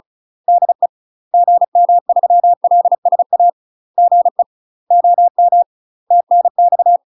Morse code, heard as a keyed tone, about 34 wpm, 700 Hz. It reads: DE GM3LSA GE OM TNX
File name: UK MASTER QSOs TRAINING / 35WPM / UK MASTER QSOs 0015